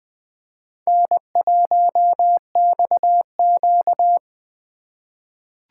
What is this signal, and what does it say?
Morse code, heard as a keyed tone, about 20 wpm, 700 Hz. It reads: N1XQ
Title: USA Call Signs - 4 Characters - 4x Character Spacing - 20wpm